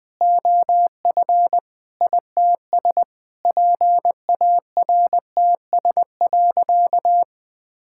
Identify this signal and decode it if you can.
Morse code, heard as a keyed tone, about 20 wpm, 700 Hz. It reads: OF ITS PARTS.
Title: Sentences with 8-Word Phrases - 20wpm